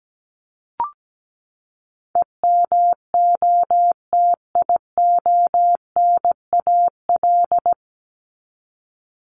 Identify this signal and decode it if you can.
Morse code, heard as a keyed tone, about 17 wpm, 700 Hz. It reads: EMOTIONAL
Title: Top 501-750 Words 17wpm